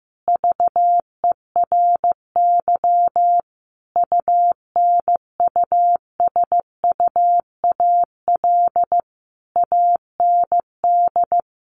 Morse code, heard as a keyed tone, about 15 wpm, 700 Hz. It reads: VERY UNUSUAL AND
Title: Sentences from Top 1400 Words 15wpm